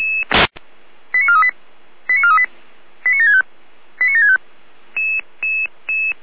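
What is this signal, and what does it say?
Five-tone selective call (ZVEI), heard as quick runs of five tones, tones of about 80 ms. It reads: 89438 89438 89765 89765
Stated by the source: ZVEI five-tone selective call received by SDR.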